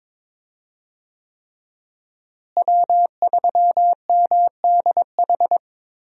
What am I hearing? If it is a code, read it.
Morse code, heard as a keyed tone, about 22 wpm, 700 Hz. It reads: W3MDH
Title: USA Call Signs - 5 Characters - Repeated 3x - 22wpm